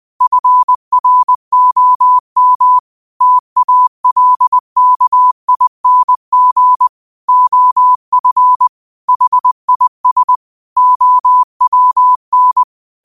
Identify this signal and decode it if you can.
Morse code, heard as a keyed tone, about 20 wpm, 1000 Hz. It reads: FROM TALKING OF HIS OWN
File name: qotdfiles / 20wpm